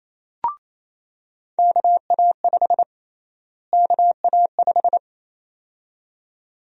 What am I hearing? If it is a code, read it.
Morse code, heard as a keyed tone, about 28 wpm, 700 Hz. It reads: KA5 KA5
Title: USA Call Sign Prefixes - Repeated 2x - 28wpm